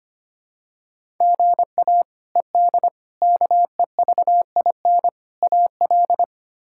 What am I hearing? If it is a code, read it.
Morse code, heard as a keyed tone, about 25 wpm, 700 Hz. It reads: GA ED KEVIN AL
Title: K1USN SST QSO Practice - 25wpm